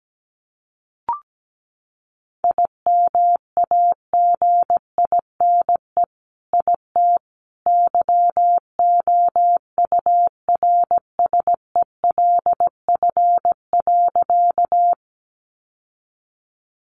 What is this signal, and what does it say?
Morse code, heard as a keyed tone, about 17 wpm, 700 Hz. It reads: IMAGINE IT YOURSELF.
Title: Sentences from Top 1400 Words 17wpm